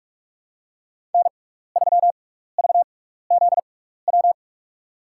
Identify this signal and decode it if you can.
Morse code, heard as a keyed tone, about 46 wpm, spaced as for 15 wpm, 690 Hz. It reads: N3VZW
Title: USA Call Signs - 5 Characters - 6x Character Spacing - 45wpm